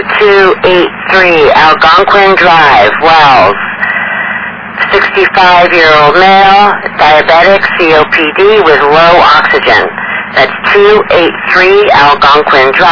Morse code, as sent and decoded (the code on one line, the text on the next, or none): none